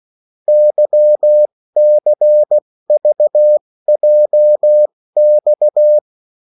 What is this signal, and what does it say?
Morse code, heard as a keyed tone, about 16 wpm, spaced as for 14 wpm, 600 Hz. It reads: YCVJX
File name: morse mp3 / revision9-vitesse 14 mots